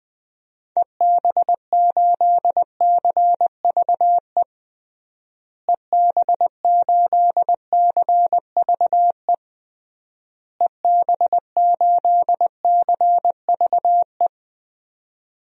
Morse code, heard as a keyed tone, about 20 wpm, 700 Hz. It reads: EB8CVE EB8CVE EB8CVE
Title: DXCC-100 Call Signs - 6 Characters - Repeated 3x - 20wpm